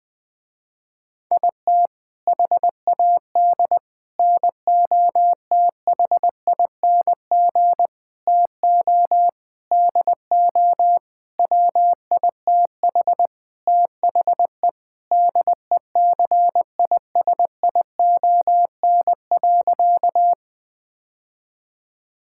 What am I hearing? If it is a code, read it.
Morse code, heard as a keyed tone, about 20 wpm, 700 Hz. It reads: IT HAD NOTHING TO DO WITH THE DECISION.
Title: Sentences with 6-Word Phrases - Spoken Beginning - 20wpm